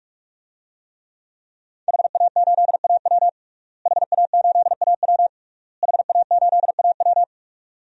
Morse code, heard as a keyed tone, about 46 wpm, 700 Hz. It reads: HA8AW HA8AW HA8AW
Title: DXCC-100 Call Signs - 5 Characters - Repeated 3x - 45wpm